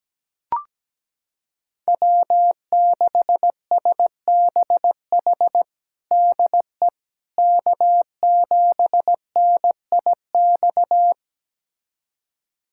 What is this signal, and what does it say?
Morse code, heard as a keyed tone, about 17 wpm, 700 Hz. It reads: W6SBH DE K7NIX
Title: USA Callsign DE Callsign 17wpm